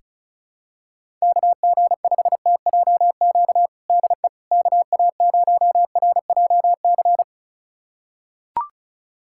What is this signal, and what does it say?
Morse code, heard as a keyed tone, about 34 wpm, 700 Hz. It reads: KG5TJQ DE KA0RJC
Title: USA Callsign DE Callsign - 6 Characters - 35wpm